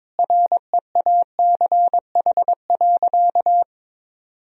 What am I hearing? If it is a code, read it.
Morse code, heard as a keyed tone, about 22 wpm, 700 Hz. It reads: REACH.
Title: Sentences from Top 1000 Words - Review Part 2 - 22wpm